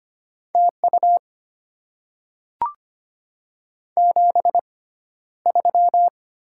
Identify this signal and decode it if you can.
Morse code, heard as a keyed tone, about 25 wpm, spaced as for 13 wpm, 700 Hz. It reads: TU 73
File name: Top 20 QSO Elements - 6x Character Spacing - 25wpm